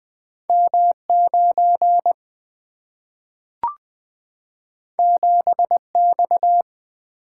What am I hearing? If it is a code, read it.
Morse code, heard as a keyed tone, about 20 wpm, 700 Hz. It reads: M 9 7 X
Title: Single-Letter Single-Number 20wpm